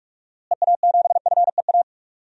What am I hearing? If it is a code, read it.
Morse code, heard as a keyed tone, about 41 wpm, 700 Hz. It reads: EA7FEA
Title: DXCC-100 Call Signs - 6 Characters - Repeated 3x - 45wpm